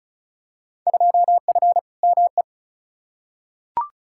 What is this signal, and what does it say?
Morse code, heard as a keyed tone, about 36 wpm, 700 Hz. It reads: 2F ME
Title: Field Day Exchanges - 35wpm